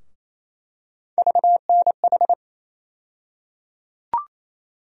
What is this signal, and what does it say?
Morse code, heard as a keyed tone, about 27 wpm, 700 Hz. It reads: VNH
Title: USA Call Sign Postfixes - 3 Letters 28wpm